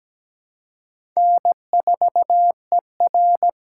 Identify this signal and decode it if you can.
Morse code, heard as a keyed tone, about 17 wpm, 700 Hz. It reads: N4ER
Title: USA Call Signs - 4 Characters - 4x Character Spacing - 17wpm